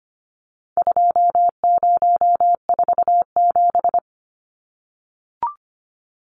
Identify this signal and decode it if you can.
Morse code, heard as a keyed tone, about 25 wpm, 700 Hz. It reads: 2047
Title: Numbers - 4-Digit 25wpm